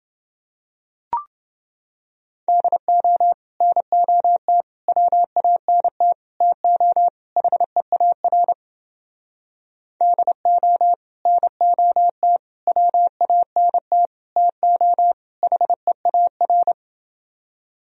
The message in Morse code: DO NOT WANT TO HEAR DO NOT WANT TO HEAR

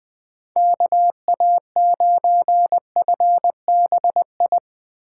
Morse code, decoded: KA9FBI

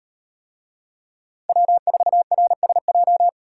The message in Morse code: W4RSJ

